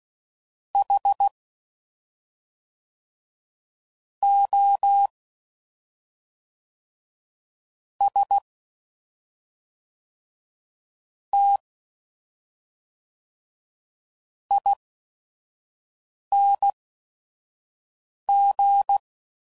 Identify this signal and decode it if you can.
Morse code, heard as a keyed tone, about 16 wpm, spaced as for 5 wpm, 800 Hz. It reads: H O S T ING
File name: Suffixes / Special ING's / SlowToFaster ING's